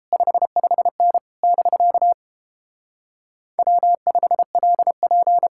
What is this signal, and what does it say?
Morse code, heard as a keyed tone, about 31 wpm, 700 Hz. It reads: 55N <BK> W5LP